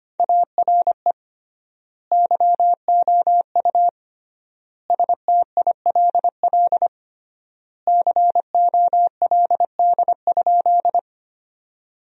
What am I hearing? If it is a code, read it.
Morse code, heard as a keyed tone, about 25 wpm, 700 Hz. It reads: ARE YOU STILL COLD?
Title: 3x Word-Spacing - Sentences from Top 400 Words - 25wpm